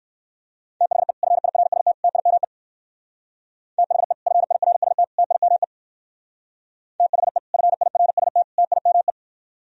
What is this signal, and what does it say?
Morse code, heard as a keyed tone, about 58 wpm, 700 Hz. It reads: THE FIRST TIME THE FIRST TIME THE FIRST TIME